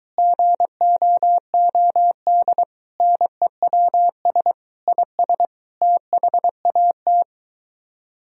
Morse code, heard as a keyed tone, about 23 wpm, 700 Hz. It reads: GOOD NEWS IS THAT